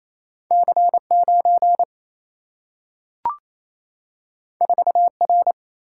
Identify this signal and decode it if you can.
Morse code, heard as a keyed tone, about 28 wpm, 700 Hz. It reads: C9 4R
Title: Single-Letter Single-Number 28wpm